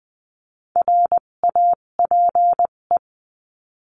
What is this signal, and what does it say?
Morse code, heard as a keyed tone, about 20 wpm, spaced as for 17 wpm, 700 Hz. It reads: RAPE
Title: 257 Lesson 42b - Words 14-20wpm